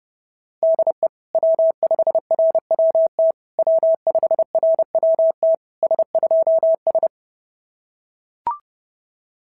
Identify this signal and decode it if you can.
Morse code, heard as a keyed tone, about 30 wpm, 650 Hz. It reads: DE W5RWT W5RWT S2S
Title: SOTA QSO Practice 30wpm